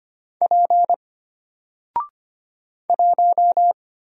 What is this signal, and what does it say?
Morse code, heard as a keyed tone, about 25 wpm, 700 Hz. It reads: P 1
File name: Single-Letter-Number-25wpm